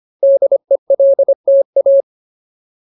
Morse code, heard as a keyed tone, about 25 wpm, 550 Hz. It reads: DELTA
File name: Ultimate Word List / 25WPM Ultimate